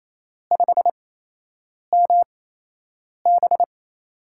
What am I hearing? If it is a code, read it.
Morse code, heard as a keyed tone, about 28 wpm, spaced as for 8 wpm, 700 Hz. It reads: 5MB
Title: USA Call Signs - 5 Characters - 8x Character Spacing - 28wpm